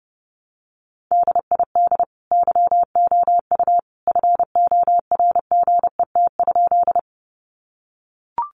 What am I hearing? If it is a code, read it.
Morse code, heard as a keyed tone, about 30 wpm, 700 Hz. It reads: DID YOU FORGET?
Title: Sentences from Top 1200 Words 30wpm